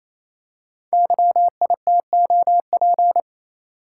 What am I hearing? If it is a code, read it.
Morse code, heard as a keyed tone, about 29 wpm, 700 Hz. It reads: YITOP
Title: Code Groups - 5 Letters - 28wpm